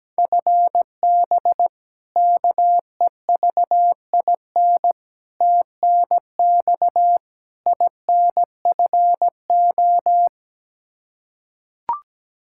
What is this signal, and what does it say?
Morse code, heard as a keyed tone, about 17 wpm, 700 Hz. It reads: FB KEVIN TNX INFO
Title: Practice QSO - Exchange 3 - More Info - 17wpm